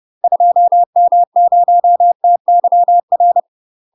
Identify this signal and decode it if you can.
Morse code, heard as a keyed tone, about 30 wpm, 700 Hz. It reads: 2M0TYR
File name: UK MASTER QSOs TRAINING / 30WPM / UK MASTER QSOs 0028